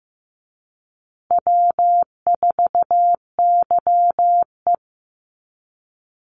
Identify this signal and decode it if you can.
Morse code, heard as a keyed tone, about 15 wpm, 700 Hz. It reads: W4YE
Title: Easy Call Signs Repeated 2x - 15wpm